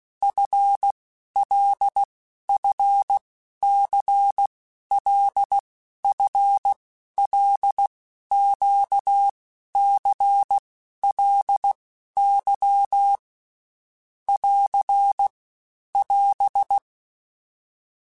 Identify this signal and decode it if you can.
Morse code, heard as a keyed tone, about 16 wpm, 800 Hz. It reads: F L F C L F L Q C L Y <AR> <AS>
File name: Basic Letter Groups By7Groups / LFCYQ / LFCYQ@15